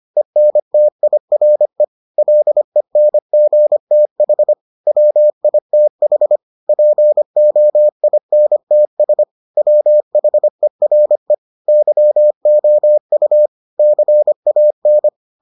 Morse code, heard as a keyed tone, about 25 wpm, 600 Hz. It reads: ENTIRE LENGTH WITH POINTS WHERE YOU CAN